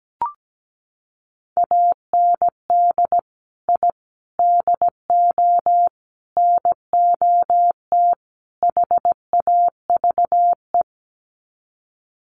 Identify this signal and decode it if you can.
Morse code, heard as a keyed tone, about 17 wpm, 700 Hz. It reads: AND I DO NOT HAVE
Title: Sets of 5 Words Encore 17wpm